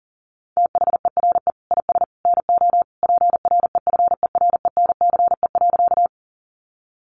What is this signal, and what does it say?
Morse code, heard as a keyed tone, about 37 wpm, 700 Hz. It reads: THERE IS NO PREFERENCE.
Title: Sentences from Top 1500 Words 40wpm